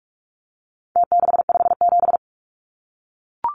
Morse code, heard as a keyed tone, about 44 wpm, 700 Hz. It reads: T657
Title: Numbers - Contest Style - 4-Digit 45wpm